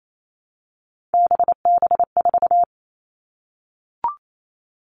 Morse code, heard as a keyed tone, about 28 wpm, 700 Hz. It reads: BB4
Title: All Cut Numbers - 3-Digit 28wpm